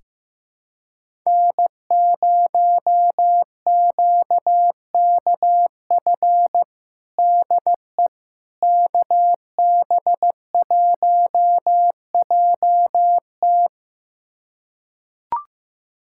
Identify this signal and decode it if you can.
Morse code, heard as a keyed tone, about 16 wpm, 700 Hz. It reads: N0QKF DE KB1JT
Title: USA Callsign DE Callsign - 5 Characters - 15wpm